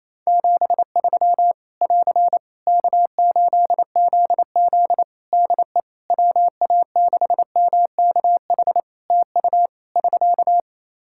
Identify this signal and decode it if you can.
Morse code, heard as a keyed tone, about 27 wpm, 700 Hz. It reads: 73 <AR> K8ZZ DE WA6MKH TU <SK>
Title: K1USN SST QSO Practice 28wpm